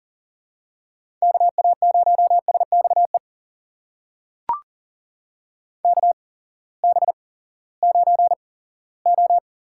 Morse code, heard as a keyed tone, about 39 wpm, spaced as for 20 wpm, 700 Hz. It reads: KA0SXE KB9O
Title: USA Call Signs - 6 Characters - 8x Character Spacing - 40wpm